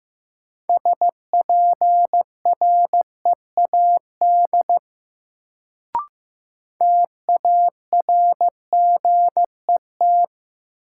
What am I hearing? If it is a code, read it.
Morse code, heard as a keyed tone, about 15 wpm, 700 Hz. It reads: SPREAD TARGET
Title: Sentences from Top 1200 Words - New Word Drill - 15wpm